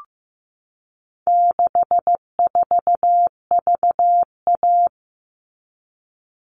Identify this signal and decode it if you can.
Morse code, heard as a keyed tone, about 15 wpm, 700 Hz. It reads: 64VA
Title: All Cut Numbers - 4-Digit 15wpm